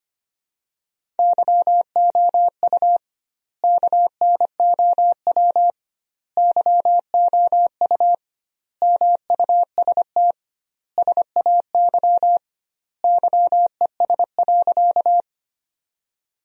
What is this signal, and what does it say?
Morse code, heard as a keyed tone, about 25 wpm, 700 Hz. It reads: YOU KNOW YOU MUST SAY YES.
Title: 2x Word-Spacing - Sentences from Top 500 Words - 25wpm